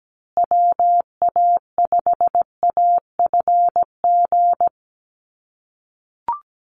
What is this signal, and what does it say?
Morse code, heard as a keyed tone, about 17 wpm, 700 Hz. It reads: WA5AFG
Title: USA Call Signs Repeated 2x 17wpm